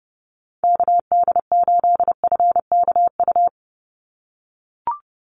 Morse code, heard as a keyed tone, about 30 wpm, 700 Hz. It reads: KD8FKU
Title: USA Call Signs 30wpm